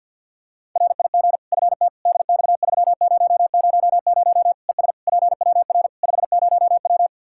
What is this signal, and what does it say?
Morse code, heard as a keyed tone, about 50 wpm, 700 Hz. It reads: RIG FT DX3000 ES PWR 50W
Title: Practice QSO - Exchange 2 - Info - 50wpm